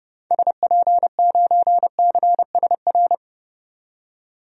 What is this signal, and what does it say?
Morse code, heard as a keyed tone, about 31 wpm, 700 Hz. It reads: SP9CSR
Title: DXCC-100 Call Signs - 6 Characters - 30wpm